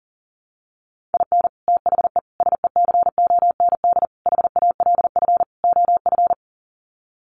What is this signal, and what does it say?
Morse code, heard as a keyed tone, about 37 wpm, 700 Hz. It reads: IN THE SECOND HALF OF
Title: Sets of 5 Words 40wpm